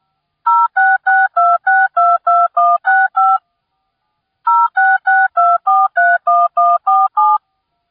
DTMF, heard as telephone touch tones, tones of about 220 ms. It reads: 0662622165 0662431147